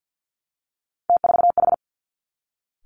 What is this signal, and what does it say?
Morse code, heard as a keyed tone, about 50 wpm, 700 Hz. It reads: T4H
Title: Three Challenging Letter-Number Combinations 50wpm